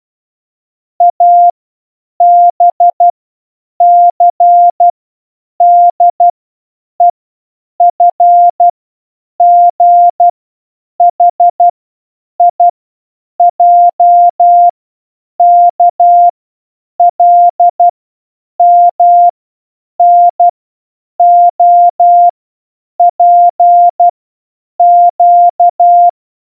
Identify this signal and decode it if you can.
Morse code, heard as a keyed tone, about 12 wpm, 700 Hz. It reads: A B C D E F G H I J K L M N O P Q
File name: A through Z in Morse code